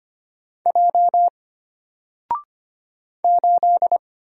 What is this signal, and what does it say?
Morse code, heard as a keyed tone, about 25 wpm, 700 Hz. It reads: J 8
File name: Single-Letter-Number-25wpm